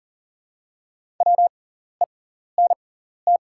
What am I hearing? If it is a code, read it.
Morse code, heard as a keyed tone, about 40 wpm, spaced as for 14 wpm, 690 Hz. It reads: WENT
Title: 6x Character Spacing - Top 200 Words - 40wpm